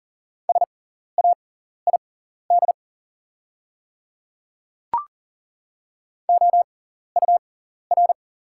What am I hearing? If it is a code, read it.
Morse code, heard as a keyed tone, about 40 wpm, spaced as for 14 wpm, 690 Hz. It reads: SAID OUR
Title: 6x Character Spacing - Top 500 Words - 40wpm